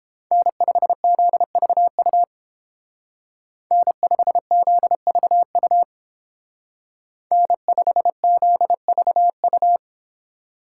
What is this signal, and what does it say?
Morse code, heard as a keyed tone, about 28 wpm, 700 Hz. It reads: N5ZVU N5ZVU N5ZVU